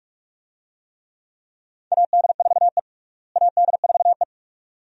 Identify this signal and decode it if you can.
Morse code, heard as a keyed tone, about 42 wpm, 700 Hz. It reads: AD4E AD4E
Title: USA Call Signs - 4 Characters - Repeated 2x - 45wpm